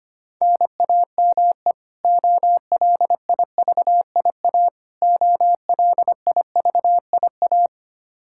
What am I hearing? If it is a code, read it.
Morse code, heard as a keyed tone, about 25 wpm, 700 Hz. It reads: NAME OLIVIA OLIVIA
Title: Practice QSO - Exchange 1 - Beginning - 25wpm